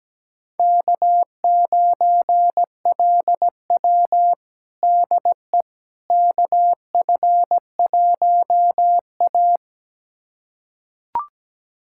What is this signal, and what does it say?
Morse code, heard as a keyed tone, about 17 wpm, 700 Hz. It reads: K9LW DE KF1A
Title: USA Callsign DE Callsign - 4 Characters - 17wpm